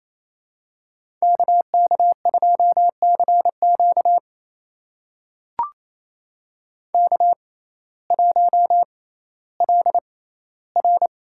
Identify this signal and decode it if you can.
Morse code, heard as a keyed tone, about 28 wpm, spaced as for 15 wpm, 700 Hz. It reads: KK2CQ K1LR
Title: USA Call Signs - 5 Characters - 6x Character Spacing - 28wpm